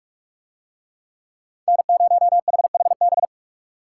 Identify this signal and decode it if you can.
Morse code, heard as a keyed tone, about 42 wpm, 700 Hz. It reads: N0HHB